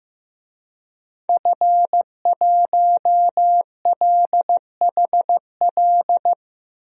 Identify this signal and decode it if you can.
Morse code, heard as a keyed tone, about 15 wpm, 700 Hz. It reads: F1LHL
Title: DXCC-100 Call Signs - 5 Characters - 15wpm